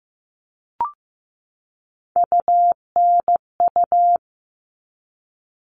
Morse code, heard as a keyed tone, about 15 wpm, 700 Hz. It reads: UNU
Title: All Cut Numbers - 3-Digit 15wpm